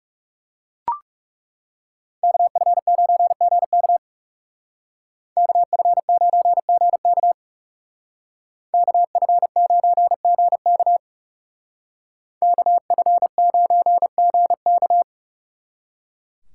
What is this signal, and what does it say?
Morse code, heard as a keyed tone, about 36 wpm, 700 Hz. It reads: KF9GK KF9GK KF9GK KF9GK
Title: Speed Racing - USA 2x2 Call Signs 30wpm